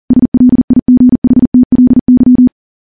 Morse code, heard as a keyed tone, about 40 wpm, 250 Hz. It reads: SLIGHTLY
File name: morse / morse-training-40WPM